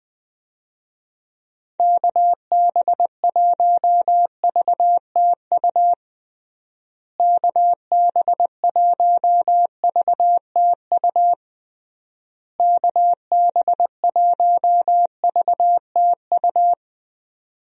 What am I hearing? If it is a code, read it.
Morse code, heard as a keyed tone, about 21 wpm, 700 Hz. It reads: KB1VTU KB1VTU KB1VTU
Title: USA Call Signs - 6 Characters - Repeated 3x - 20wpm